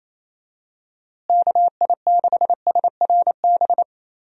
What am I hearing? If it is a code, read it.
Morse code, heard as a keyed tone, about 27 wpm, 700 Hz. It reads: KI6SRB
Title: USA Call Signs - 6 Characters - Repeated 3x - 28wpm